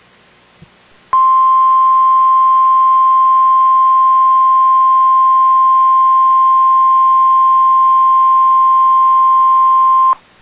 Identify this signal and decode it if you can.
Emergency Alert System (SAME) tones (the 1050 Hz attention tone). It is heard from 1.1 to 10.2 s.